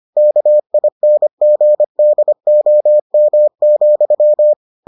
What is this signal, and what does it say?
Morse code, heard as a keyed tone, about 25 wpm, 600 Hz. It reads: KINGDOM,